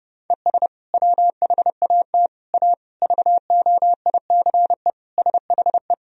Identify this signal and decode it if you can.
Morse code, heard as a keyed tone, about 30 wpm, 700 Hz. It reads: ES WHAT A VOICE SHE
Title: Sentences from Top 500 Words and Abbreviations 30wpm